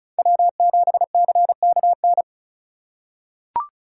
Morse code, heard as a keyed tone, about 34 wpm, 700 Hz. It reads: W7CKN
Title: USA Call Signs - 5 Characters - 3x Character Spacing - 35wpm